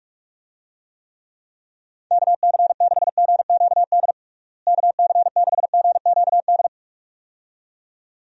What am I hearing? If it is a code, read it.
Morse code, heard as a keyed tone, about 47 wpm, 700 Hz. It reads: KC6GQD KC6GQD
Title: USA Call Signs - 6 Characters - Repeated 2x - 45wpm